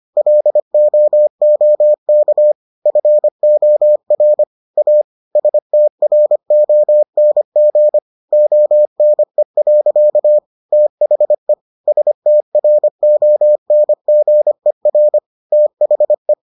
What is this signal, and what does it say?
Morse code, heard as a keyed tone, about 25 wpm, 600 Hz. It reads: LOOK FOR A STRONG ONE. THE STRONGER THE